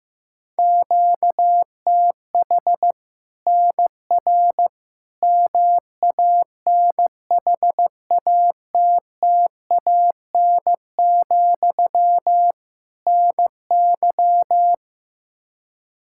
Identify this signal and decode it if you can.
Morse code, heard as a keyed tone, about 15 wpm, 700 Hz. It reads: QTH NR MANHATTAN, NY
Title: Practice QSO - Exchange 1 - Beginning - 15wpm